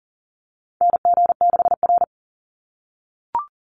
Morse code, heard as a keyed tone, about 40 wpm, 700 Hz. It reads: NG6R
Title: NA SOTA Chaser Callsigns 40wpm